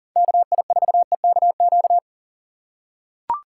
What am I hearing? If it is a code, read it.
Morse code, heard as a keyed tone, about 40 wpm, 700 Hz. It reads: KI4EKQ
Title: USA Call Signs - 4x Character Spacing - 40wpm